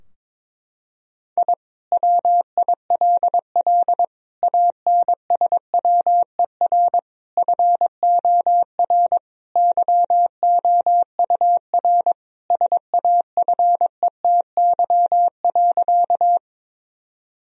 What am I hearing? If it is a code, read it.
Morse code, heard as a keyed tone, about 22 wpm, 700 Hz. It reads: I WILL ANSWER FOR YOUR SAFETY.
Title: Sentences from Top 1000 Words - Review Part 4 - 22wpm